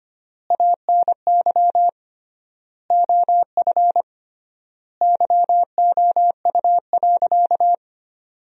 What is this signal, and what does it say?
Morse code, heard as a keyed tone, about 25 wpm, 700 Hz. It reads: ANY OF YOU.
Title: 3x Word-Spacing - Sentences from Top 400 Words - 25wpm